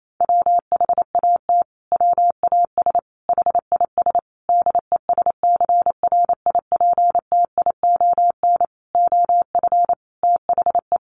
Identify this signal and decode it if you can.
Morse code, heard as a keyed tone, about 28 wpm, 700 Hz. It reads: WHAT WAS HIS DESCRIPTION OF THE
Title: Sentences from Top 1300 Words 28wpm